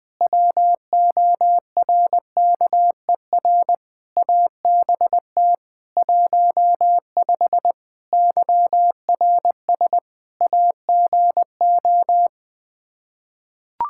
Morse code, heard as a keyed tone, about 20 wpm, 700 Hz. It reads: WORKER ABT 15 YRS AGO
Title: Practice QSO - Exchange 3 - More Info - 20wpm